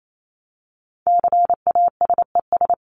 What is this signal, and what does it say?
Morse code, heard as a keyed tone, about 28 wpm, 700 Hz. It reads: CASES
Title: ES:S Suffix - Top 100 Words with Base Words 28wpm